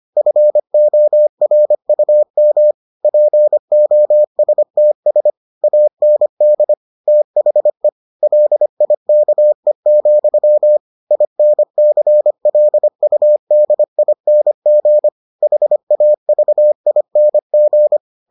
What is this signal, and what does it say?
Morse code, heard as a keyed tone, about 25 wpm, 600 Hz. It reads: FORUM POSTS AND THE LIKE, INCLUDING HAVING